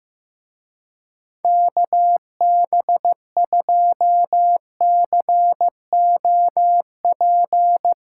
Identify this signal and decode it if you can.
Morse code, heard as a keyed tone, about 17 wpm, 700 Hz. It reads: KB2COP
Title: Speed Racing - USA 2x3 Call Signs 15wpm